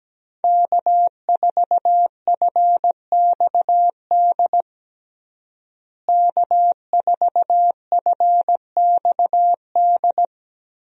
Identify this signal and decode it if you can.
Morse code, heard as a keyed tone, about 17 wpm, 700 Hz. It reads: K4FXD K4FXD
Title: USA Call Signs - 5 Characters - Repeated 3x - 17wpm